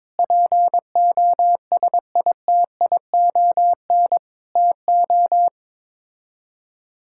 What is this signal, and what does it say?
Morse code, heard as a keyed tone, about 22 wpm, 700 Hz. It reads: POSITION TO
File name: practice / 22wpm / Sets of 7 Words - 22wpm